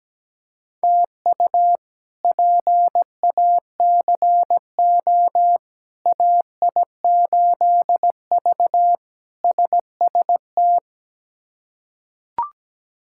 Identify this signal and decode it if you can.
Morse code, heard as a keyed tone, about 17 wpm, 700 Hz. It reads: TU PACO AI8V SST
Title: K1USN SST QSO Practice - 17wpm